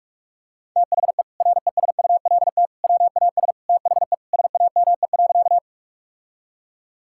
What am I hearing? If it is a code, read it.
Morse code, heard as a keyed tone, about 46 wpm, 700 Hz. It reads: THE RESULT WAS THE SAME.
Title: Sentences from Top 1000 Words - Review Part 5 - 45wpm